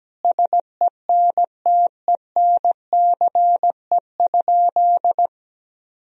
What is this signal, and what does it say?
Morse code, heard as a keyed tone, about 17 wpm, 700 Hz. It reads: SENTENCE?
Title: Sentences from Top 1000 Words - Review Part 1 - 17wpm